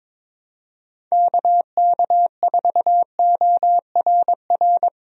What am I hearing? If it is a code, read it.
Morse code, heard as a keyed tone, about 22 wpm, 700 Hz. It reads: KK4ORR
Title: USA Call Signs - 6 Characters - 22wpm